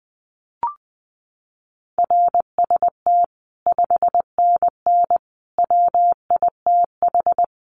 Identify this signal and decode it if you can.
Morse code, heard as a keyed tone, about 20 wpm, 700 Hz. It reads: RST 5NN WITH
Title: Common QSO Phrases 20wpm